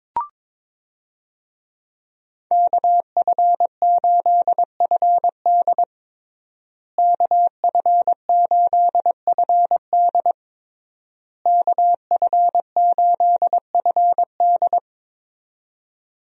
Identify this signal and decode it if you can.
Morse code, heard as a keyed tone, about 22 wpm, 700 Hz. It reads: KF8FD KF8FD KF8FD
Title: USA Call Signs - 5 Characters - Repeated 3x - 22wpm